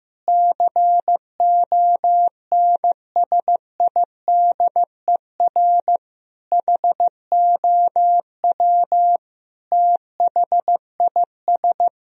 Morse code, heard as a keyed tone, about 15 wpm, 700 Hz. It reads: CONSIDER HOW THIS
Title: Sentences from Top 1000 Words - Review Part 4 - 15wpm